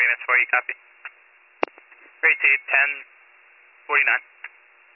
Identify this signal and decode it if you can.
Emergency Alert System (SAME) tones (header bursts), heard as harsh buzzing bursts, with speech. It is heard from 0.2 to 4.6 s.